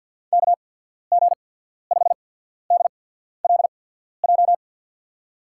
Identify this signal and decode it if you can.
Morse code, heard as a keyed tone, about 52 wpm, 700 Hz. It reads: K G 5 D L J
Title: USA Call Signs - 6 Characters - 8x Character Spacing - 50wpm